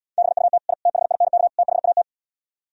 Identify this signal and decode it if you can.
Morse code, heard as a keyed tone, about 77 wpm, 700 Hz. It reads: BUT I THINK THAT